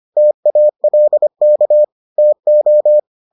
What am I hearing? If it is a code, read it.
Morse code, heard as a keyed tone, about 25 wpm, 600 Hz. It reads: TALK TO